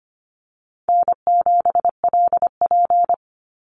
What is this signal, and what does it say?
Morse code, heard as a keyed tone, about 25 wpm, 700 Hz. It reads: N7LP
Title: NA SOTA Chaser Callsigns 25wpm